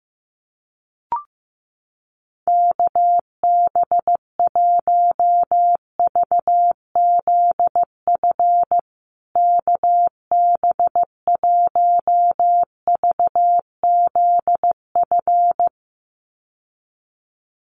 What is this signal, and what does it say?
Morse code, heard as a keyed tone, about 14 wpm, 700 Hz. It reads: KB1VZF KB1VZF
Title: USA Call Signs Repeated 2x 15wpm